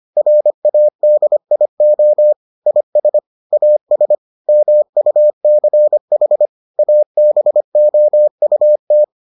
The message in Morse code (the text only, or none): RADIO IS AS MUCH ABOUT